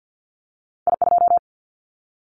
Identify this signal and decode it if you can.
Morse code, heard as a keyed tone, about 50 wpm, 700 Hz. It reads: I2
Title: Callsign Prefixes for Top 50 DXCC - 50wpm